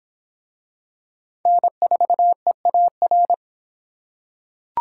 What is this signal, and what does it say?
Morse code, heard as a keyed tone, about 26 wpm, 700 Hz. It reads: N4EAR